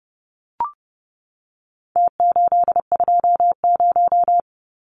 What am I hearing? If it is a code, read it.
Morse code, heard as a keyed tone, about 30 wpm, 700 Hz. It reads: T820
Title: Numbers - Contest Style - 4-Digit 30wpm